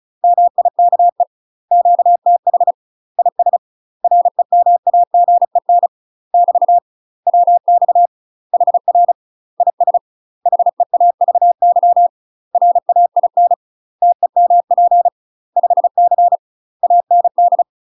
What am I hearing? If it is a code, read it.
Morse code, heard as a keyed tone, about 34 wpm, 700 Hz. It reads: MIKE QTH IS REMAGEN <BT> WX HR IS HEAVY RAIN TEMP 5C AND
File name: UK MASTER QSOs TRAINING / 35WPM / UK MASTER QSOs 0039